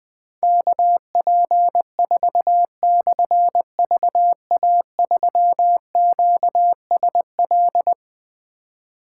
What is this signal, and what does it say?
Morse code, heard as a keyed tone, about 20 wpm, 700 Hz. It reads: KP4/VA3QSL
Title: Difficult Call Signs 20wpm